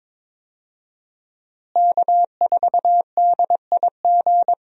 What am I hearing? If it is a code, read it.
Morse code, heard as a keyed tone, about 22 wpm, 700 Hz. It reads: K4DIG
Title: USA Call Signs - 5 Characters - Repeated 2x - 22wpm